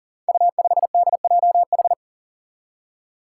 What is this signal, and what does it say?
Morse code, heard as a keyed tone, about 40 wpm, 700 Hz. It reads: U5DJH
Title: Five Challenging Letter-Number Combinations 40wpm